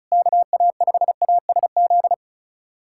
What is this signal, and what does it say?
Morse code, heard as a keyed tone, about 36 wpm, 700 Hz. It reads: KA5ASZ
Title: USA Callsign DE Callsign - 6 Characters - 35wpm